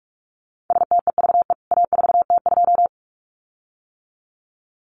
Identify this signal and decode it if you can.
Morse code, heard as a keyed tone, about 44 wpm, 700 Hz. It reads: STEVE A4T2
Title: CWT Exchange 45wpm